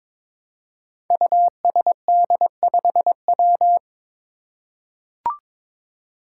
Morse code, heard as a keyed tone, about 22 wpm, 700 Hz. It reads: USD5W
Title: Five Challenging Letter-Number Combinations 22wpm